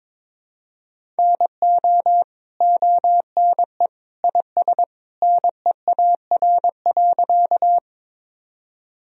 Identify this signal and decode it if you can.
Morse code, heard as a keyed tone, about 22 wpm, 700 Hz. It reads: NO ONE IS NEAR.
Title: Sentences from Top 1000 Words - Review Part 1 - 22wpm